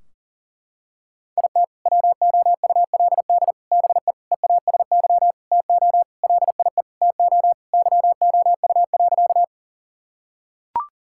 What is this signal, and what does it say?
Morse code, heard as a keyed tone, about 40 wpm, 700 Hz. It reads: IT WOULD BE EASY TO LIE TO YOU.
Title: Sentences from Top 1600 Words 40wpm